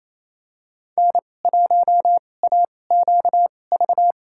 Morse code, heard as a keyed tone, about 28 wpm, spaced as for 20 wpm, 700 Hz. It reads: N1AQV
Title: USA Call Signs - 5 Characters - 2x Character Spacing - 28wpm